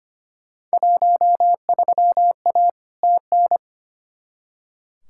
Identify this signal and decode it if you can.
Morse code, heard as a keyed tone, about 25 wpm, 700 Hz. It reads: 13A TN
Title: Field Day Exchanges Encore - 25wpm